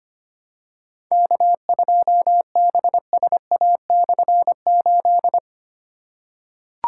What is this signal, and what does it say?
Morse code, heard as a keyed tone, about 25 wpm, 700 Hz. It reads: K2BSA/8
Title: K1USN SST QSO Practice - 25wpm